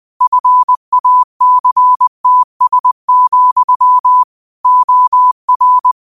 Morse code, heard as a keyed tone, about 20 wpm, 1000 Hz. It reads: FACTS, OR
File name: qotdfiles / 20wpm